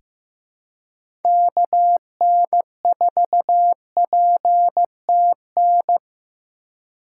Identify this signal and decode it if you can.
Morse code, heard as a keyed tone, about 16 wpm, 700 Hz. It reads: KN4PTN
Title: USA Call Signs - 4x Character Spacing - 15wpm